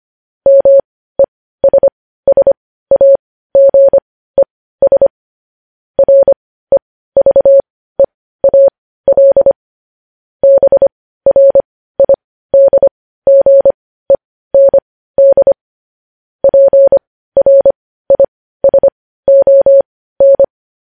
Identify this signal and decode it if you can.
Morse code, heard as a keyed tone, about 25 wpm, spaced as for 15 wpm, 550 Hz. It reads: MESSAGES REVEAL BRIDGEND PRISON
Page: news headlines in morse code at 15 wpm